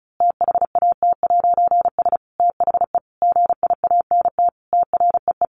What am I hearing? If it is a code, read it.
Morse code, heard as a keyed tone, about 34 wpm, 700 Hz. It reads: THAT'S THE GIANT TREE
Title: A Sound Of Thunder 35wpm